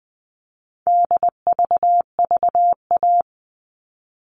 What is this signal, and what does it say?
Morse code, heard as a keyed tone, about 20 wpm, 700 Hz. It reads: DVVA
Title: All Cut Numbers - 4-Digit 20wpm